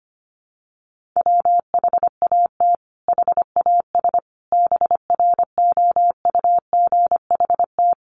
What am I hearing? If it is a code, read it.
Morse code, heard as a keyed tone, about 25 wpm, 700 Hz. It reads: WHAT HAS BROUGHT
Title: Sentences from Top 1500 Words 25wpm